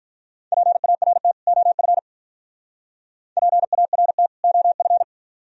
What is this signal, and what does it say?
Morse code, heard as a keyed tone, about 42 wpm, 700 Hz. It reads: PART OF PART OF
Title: Speed-Racing - Sets of 2 Words 35wpm